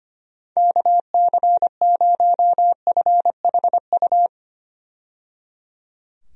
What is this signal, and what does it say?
Morse code, heard as a keyed tone, about 25 wpm, 700 Hz. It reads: KC0FHU
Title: USA Call Signs - 6 Characters - 25wpm